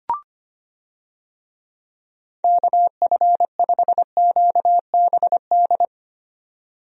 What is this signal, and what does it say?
Morse code, heard as a keyed tone, about 25 wpm, 700 Hz. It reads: KF5QBD
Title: USA Call Signs - 6 Characters - 25wpm